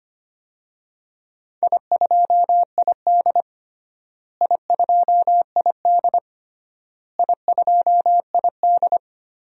Morse code, heard as a keyed tone, about 25 wpm, 700 Hz. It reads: I2ID I2ID I2ID
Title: DXCC-100 Call Signs - 4 Characters - Repeated 3x - 25wpm